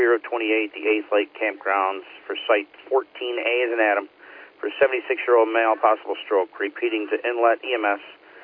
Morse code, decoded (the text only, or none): none